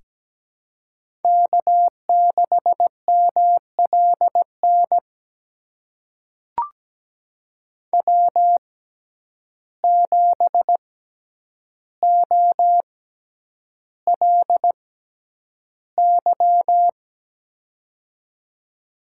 Morse code, heard as a keyed tone, about 17 wpm, spaced as for 9 wpm, 700 Hz. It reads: K6MLN W7OLY
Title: USA Call Signs - 5 Characters - 6x Character Spacing - 17wpm